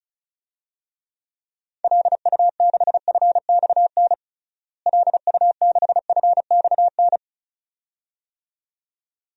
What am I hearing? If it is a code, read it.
Morse code, heard as a keyed tone, about 36 wpm, 700 Hz. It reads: LU6FXN LU6FXN
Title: DXCC-100 Call Signs - 6 Characters - Repeated 2x - 35wpm